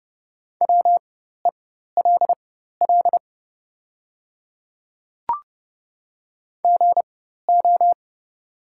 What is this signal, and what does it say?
Morse code, heard as a keyed tone, about 30 wpm, spaced as for 14 wpm, 700 Hz. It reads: WELL GO
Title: 4x Character Spacing - Top 300 Words - 30wpm